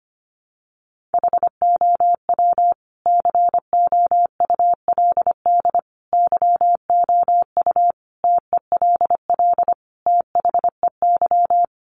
Morse code, heard as a keyed tone, about 25 wpm, 700 Hz. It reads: HOW COULD YOU TELL THEY
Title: Sentences from Top 1000 Words 25wpm